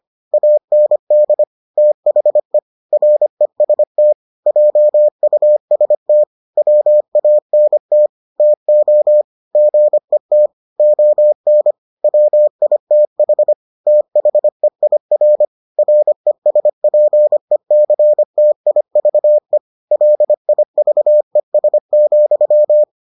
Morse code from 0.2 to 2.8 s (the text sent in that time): AND THE